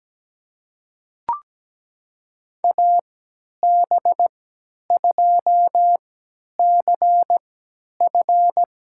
Morse code, heard as a keyed tone, about 17 wpm, spaced as for 10 wpm, 700 Hz. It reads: AB2CF
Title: USA Call Signs - 5 Characters - 3x Character Spacing - 17wpm